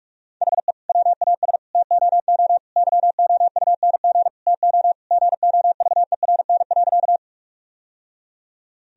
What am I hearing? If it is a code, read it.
Morse code, heard as a keyed tone, about 46 wpm, 700 Hz. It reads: HE WAS TOO YOUNG TO GOVERN.
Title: Sentences from Top 1000 Words - Review Part 2 - 45wpm